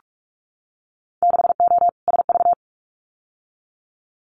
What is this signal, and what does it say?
Morse code, heard as a keyed tone, about 46 wpm, 700 Hz. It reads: 6O SV
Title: Winter Field Day Exchanges Encore 45wpm